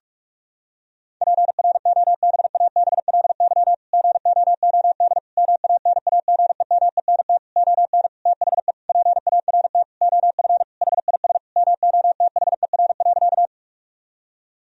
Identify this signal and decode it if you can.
Morse code, heard as a keyed tone, about 46 wpm, 700 Hz. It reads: PROBABLY GOOD MANAGEMENT ON THE PART OF HIS MOTHER.
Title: Sentences from Top 1000 Words - Review Part 4 - 45wpm